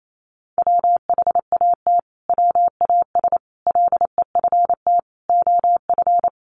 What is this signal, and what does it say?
Morse code, heard as a keyed tone, about 28 wpm, 700 Hz. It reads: WHAT WAS LEFT OF
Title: Sentences from Top 300 Words 28wpm